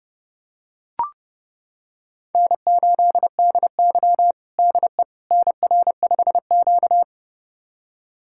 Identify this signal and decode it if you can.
Morse code, heard as a keyed tone, about 30 wpm, 700 Hz. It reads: N8DY DE NR5Q
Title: USA Callsign DE Callsign - 4 Characters - 30wpm